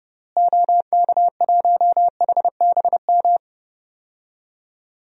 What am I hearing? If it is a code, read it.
Morse code, heard as a keyed tone, about 30 wpm, 700 Hz. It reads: OK1HBM
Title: DXCC-100 Call Signs - 6 Characters - Repeated 2x - 30wpm